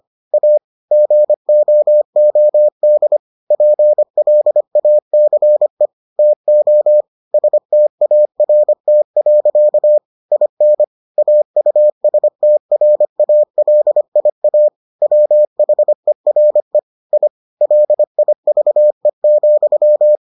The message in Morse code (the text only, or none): A GOOD PLACE TO START. IN AUSTRALIA WHERE I LIVE,